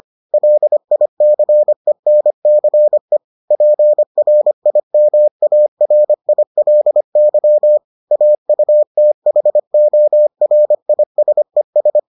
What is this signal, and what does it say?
Morse code, heard as a keyed tone, about 25 wpm, 600 Hz. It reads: LICENCE PRIMARILY AUTHORISES